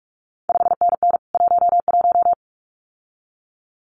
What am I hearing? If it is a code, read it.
Morse code, heard as a keyed tone, about 46 wpm, 700 Hz. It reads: 5NN 11
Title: CQ WW DX Contest Exchange Encore - 45wpm